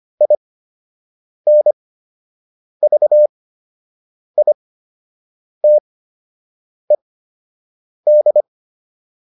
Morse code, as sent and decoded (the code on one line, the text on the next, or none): .. -. ...- .. - . -..
INVITED